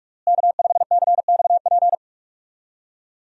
Morse code, heard as a keyed tone, about 46 wpm, 700 Hz. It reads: K5CXP
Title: USA Call Signs - 5 Characters - Repeated 3x - 45wpm